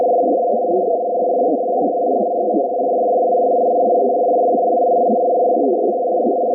Two-tone teleprinter signal, about 200 Hz shift.